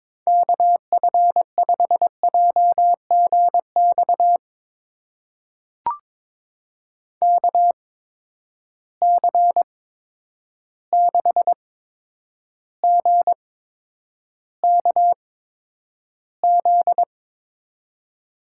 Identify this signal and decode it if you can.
Morse code, heard as a keyed tone, about 22 wpm, spaced as for 9 wpm, 700 Hz. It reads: KF5JGX KC6GKZ
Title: USA Call Signs - 6 Characters - 8x Character Spacing - 22wpm